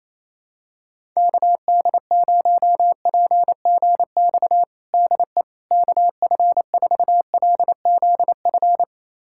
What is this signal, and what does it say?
Morse code, heard as a keyed tone, about 28 wpm, 700 Hz. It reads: KD0PGX DE KF4LZF
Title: USA Callsign DE Callsign 28wpm